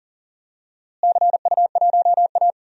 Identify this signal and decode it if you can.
Morse code, heard as a keyed tone, about 39 wpm, 700 Hz. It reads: CU1A
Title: DXCC-100 Call Signs - 4 Characters - Repeated 2x - 40wpm